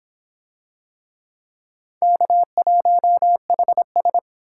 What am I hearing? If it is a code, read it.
Morse code, heard as a keyed tone, about 26 wpm, 700 Hz. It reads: K1HS